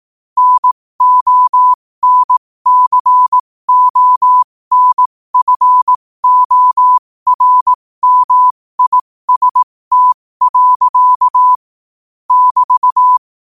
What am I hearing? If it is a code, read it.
Morse code, heard as a keyed tone, about 18 wpm, spaced as for 15 wpm, 1000 Hz. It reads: NONCONFORMIST. <BT>
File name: qotdfiles / 15wpm